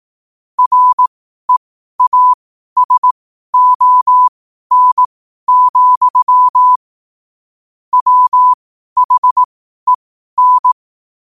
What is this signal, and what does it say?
Morse code, heard as a keyed tone, about 18 wpm, spaced as for 12 wpm, 1000 Hz. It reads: REASON, WHEN
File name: qotdfiles / 12wpm